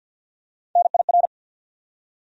Morse code, heard as a keyed tone, about 47 wpm, 700 Hz. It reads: NIR
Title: USA Call Sign Postfixes - 3 Letters 50wpm